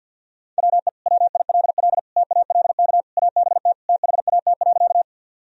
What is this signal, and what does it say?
Morse code, heard as a keyed tone, about 51 wpm, 700 Hz. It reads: WE WILL TALK ABT THAT.
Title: Sentences from Top 400 Words and Abbreviations 50wpm